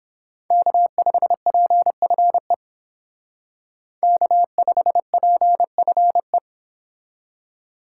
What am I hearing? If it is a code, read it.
Morse code, heard as a keyed tone, about 30 wpm, 700 Hz. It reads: K5PFE K5PFE